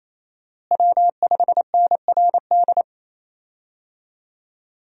Morse code, heard as a keyed tone, about 28 wpm, 700 Hz. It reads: W5NRD
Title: USA Call Signs - 5 Characters - Repeated 2x - 28wpm